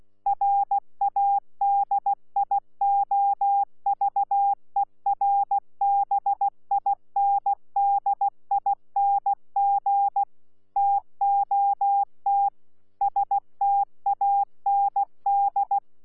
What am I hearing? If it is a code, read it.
Morse code, heard as a keyed tone, about 16 wpm, 800 Hz. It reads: RADIOVERBINDING TOT STAND
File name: Z PI4VRZ / PI4VRZ 14-jun-2025